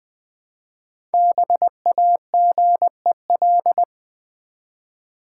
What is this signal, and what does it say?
Morse code, heard as a keyed tone, about 20 wpm, 700 Hz. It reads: BAGEL
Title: Easy Words with Hints - 20wpm